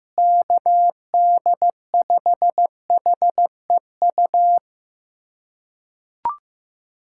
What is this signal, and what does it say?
Morse code, heard as a keyed tone, about 15 wpm, 700 Hz. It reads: KD5HEU
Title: USA Callsign DE Callsign - 6 Characters - 15wpm